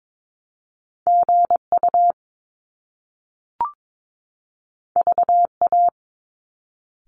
Morse code, heard as a keyed tone, about 22 wpm, 700 Hz. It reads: GU VA
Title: Top Two Letter Combinations in Words 22wpm